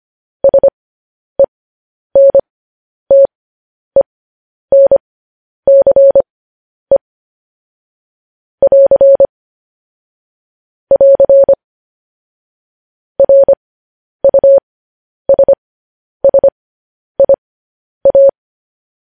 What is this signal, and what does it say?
Morse code, heard as a keyed tone, about 25 wpm, spaced as for 10 wpm, 550 Hz. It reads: SENTENCE <AR> <AR> RUSSIA
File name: podcast / News headlines-1x-25x10